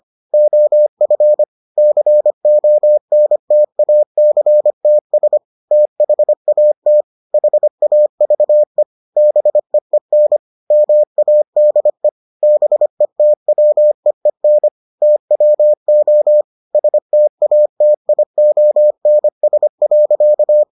Morse code, heard as a keyed tone, about 25 wpm, 600 Hz. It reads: OF CONTACTS THAT HAVE BEEN MADE BETWEEN TWO STATIONS.